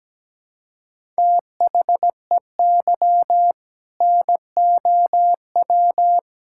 Morse code, heard as a keyed tone, about 17 wpm, 700 Hz. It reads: THEY NOW
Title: Sentences from Top 1000 Words - Review Part 5 - 17wpm